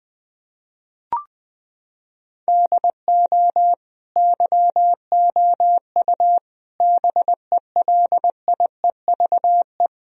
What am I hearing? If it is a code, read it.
Morse code, heard as a keyed tone, about 20 wpm, 700 Hz. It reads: DO YOU BELIEVE
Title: Sentences from Top 1000 Words - Review Part 3 - 20wpm